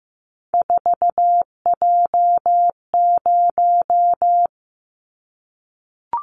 Morse code, heard as a keyed tone, about 15 wpm, 700 Hz. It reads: 4J0
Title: Callsign Prefixes for Top 100 DXCC - 15wpm